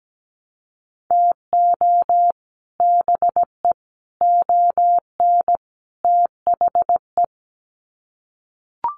Morse code, heard as a keyed tone, about 17 wpm, 700 Hz. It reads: TO BE ON THE
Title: Sets of 4 Words 17wpm